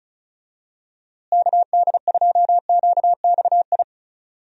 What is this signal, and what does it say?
Morse code, heard as a keyed tone, about 34 wpm, 700 Hz. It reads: KD2QXI
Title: USA Call Signs - 6 Characters - Repeated 3x - 35wpm